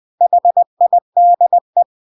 Morse code, heard as a keyed tone, about 20 wpm, 700 Hz. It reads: HIDE
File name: CW / jodorowsky / 82 commandments 20-20